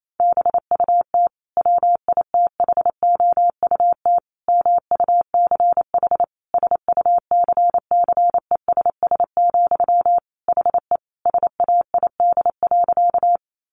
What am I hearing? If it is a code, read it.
Morse code, heard as a keyed tone, about 28 wpm, 700 Hz. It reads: BUT WITHOUT MUCH SUCCESS, HE SAID.
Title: Sentences from Top 900 Words 28wpm